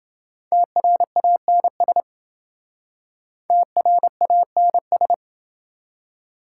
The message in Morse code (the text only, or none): TRANS TRANS